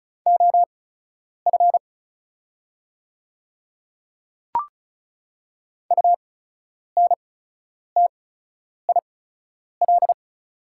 Morse code, heard as a keyed tone, about 36 wpm, spaced as for 10 wpm, 700 Hz. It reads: OF UNTIL
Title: 8x Character Spacing - Top 500 Words - 35wpm